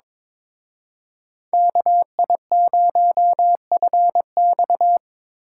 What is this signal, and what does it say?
Morse code, heard as a keyed tone, about 26 wpm, 700 Hz. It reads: KI0FX
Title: Speed Racing - USA Call Signs 22wpm